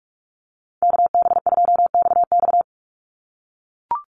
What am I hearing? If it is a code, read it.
Morse code, heard as a keyed tone, about 44 wpm, 700 Hz. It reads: KB2XX
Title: NA SOTA Chaser Callsigns 45wpm